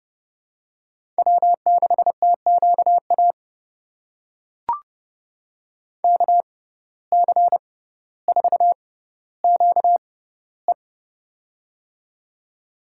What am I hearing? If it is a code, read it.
Morse code, heard as a keyed tone, about 30 wpm, spaced as for 15 wpm, 700 Hz. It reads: W6TQA KC4QE
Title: USA Call Signs - 5 Characters - 6x Character Spacing - 30wpm